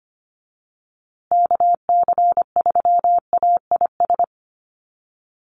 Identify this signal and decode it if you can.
Morse code, heard as a keyed tone, about 25 wpm, 700 Hz. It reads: KC3AIS
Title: USA Call Signs Repeated 3x 25wpm